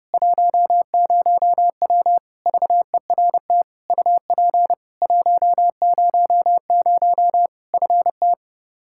Morse code, heard as a keyed tone, about 30 wpm, 700 Hz. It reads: 10W VERT UP 100 FT
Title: Practice QSO - Exchange 2 - Info - 30wpm